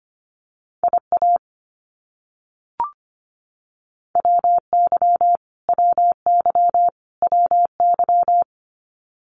Morse code, heard as a keyed tone, about 25 wpm, 700 Hz. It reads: IA WY WY WY
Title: US States Repeated 3x 25wpm